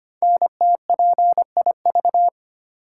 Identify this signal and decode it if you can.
Morse code, heard as a keyed tone, about 25 wpm, 700 Hz. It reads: NTPIV
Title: Code Groups - 5 Letter:Numbers - 25wpm